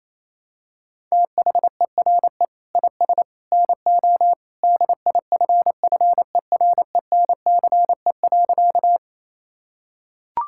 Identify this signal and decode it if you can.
Morse code, heard as a keyed tone, about 28 wpm, 700 Hz. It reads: THERE IS NO DIFFERENCE.
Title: Sentences from Top 1000 Words - Review Part 5 - 28wpm